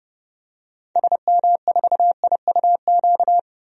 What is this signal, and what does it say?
Morse code, heard as a keyed tone, about 29 wpm, 700 Hz. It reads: SM4IUQ
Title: DXCC-100 Call Signs - 6 Characters - 30wpm